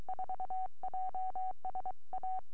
Morse code, heard as a keyed tone, about 23 wpm, 750 Hz. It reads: 4 JS A
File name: g4jha 4m